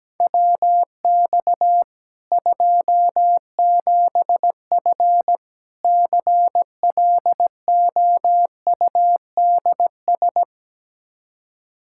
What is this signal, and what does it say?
Morse code, heard as a keyed tone, about 17 wpm, 700 Hz. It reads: WX 27F CLOUDS
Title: K1USN SST QSO Practice 17wpm